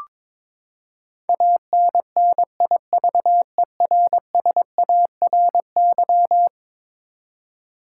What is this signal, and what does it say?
Morse code, heard as a keyed tone, about 22 wpm, 700 Hz. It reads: ANNIVERSARY
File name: practice / 22wpm / Eleven Letter Words - 22wpm